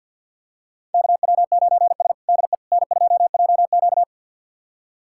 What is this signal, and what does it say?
Morse code, heard as a keyed tone, about 47 wpm, 700 Hz. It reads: KW9S DE N2JQ
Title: USA Callsign DE Callsign - 4 Characters - 50wpm